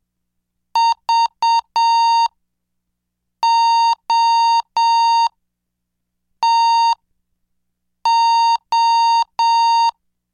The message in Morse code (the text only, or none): VOTO